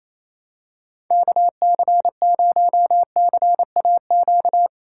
Morse code, heard as a keyed tone, about 28 wpm, 700 Hz. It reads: KC0CAQ
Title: USA Call Signs - 6 Characters - 3x Character Spacing - 28wpm